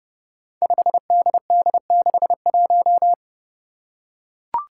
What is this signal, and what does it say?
Morse code, heard as a keyed tone, about 30 wpm, 700 Hz. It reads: 5DD61
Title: Five Challenging Letter-Number Combinations 30wpm